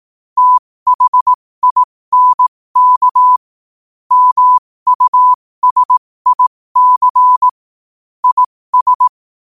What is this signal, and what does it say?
Morse code, heard as a keyed tone, about 18 wpm, spaced as for 15 wpm, 1000 Hz. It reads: THINK MUSIC IS
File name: qotdfiles / 15wpm